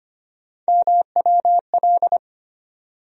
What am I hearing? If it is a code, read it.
Morse code, heard as a keyed tone, about 25 wpm, 700 Hz. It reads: MWL
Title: USA Call Sign Postfixes - 3 Letters 25wpm